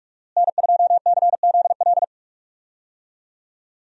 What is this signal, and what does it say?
Morse code, heard as a keyed tone, about 46 wpm, 700 Hz. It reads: N2CZL
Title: USA Call Signs - 5 Characters - Repeated 2x - 45wpm